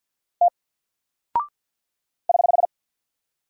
Morse code, heard as a keyed tone, about 48 wpm, 700 Hz. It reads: T <HH>
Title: Single Character (Letter:Number:Punctuation:Prosign) 50wpm